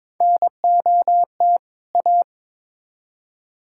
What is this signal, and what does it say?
Morse code, heard as a keyed tone, about 22 wpm, 700 Hz. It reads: NOT A
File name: Build A Phrase - 2-Word Phrases - 22wpm